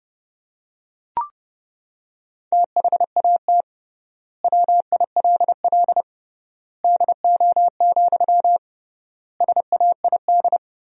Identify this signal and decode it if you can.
Morse code, heard as a keyed tone, about 30 wpm, 700 Hz. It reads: THAT WILL DO, SAID
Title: 3x Word-Spacing - Sentences from Top 100 Words - 30wpm